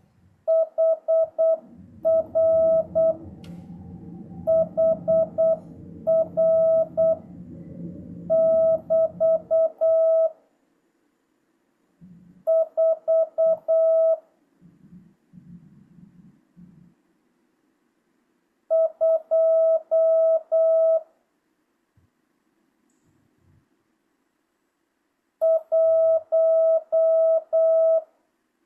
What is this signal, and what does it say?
Morse code, heard as a keyed tone, about 8 wpm, 650 Hz. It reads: HR HR <BT> 4 2 1
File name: jh8cbh / 2025cw